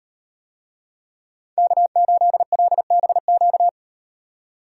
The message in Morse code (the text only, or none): K8LBQ